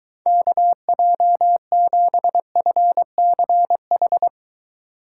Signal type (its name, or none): Morse code